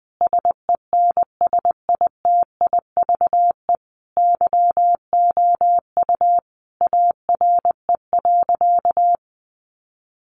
Morse code, heard as a keyed tone, about 20 wpm, 700 Hz. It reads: SENSITIVE YOU ARE.
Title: Sentences from Top 1400 Words 20wpm